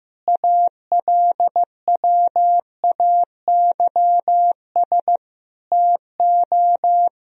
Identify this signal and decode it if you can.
Morse code, heard as a keyed tone, about 15 wpm, 700 Hz. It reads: ALWAYS TO
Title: Sentences from Top 1000 Words - Review Part 4 - 15wpm